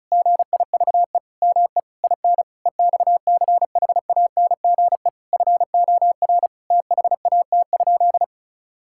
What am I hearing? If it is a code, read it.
Morse code, heard as a keyed tone, about 34 wpm, 700 Hz. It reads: GIVE ME IN EXCHANGE FOR THAT?
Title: Sentences from Top 1000 Words - Review Part 6 - 35wpm